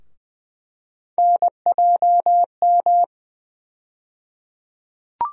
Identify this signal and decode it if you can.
Morse code, heard as a keyed tone, about 20 wpm, 700 Hz. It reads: NJM
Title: USA Call Sign Postfixes - 3 Letters 20wpm